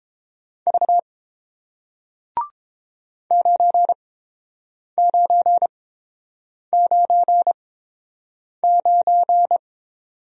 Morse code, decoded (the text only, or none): V 9999